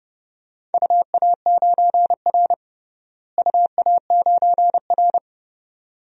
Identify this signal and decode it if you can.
Morse code, heard as a keyed tone, about 31 wpm, 700 Hz. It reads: UA9R UA9R
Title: DXCC-100 Call Signs - 4 Characters - Repeated 3x - 30wpm